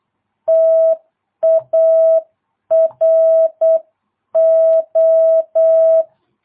Morse code, heard as a keyed tone, about 8 wpm, 650 Hz. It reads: TARO